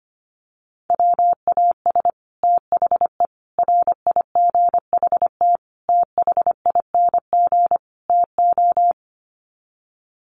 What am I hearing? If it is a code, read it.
Morse code, heard as a keyed tone, about 25 wpm, 700 Hz. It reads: WAS THE RIGHT THING TO
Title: Sets of 5 Words Encore2 25wpm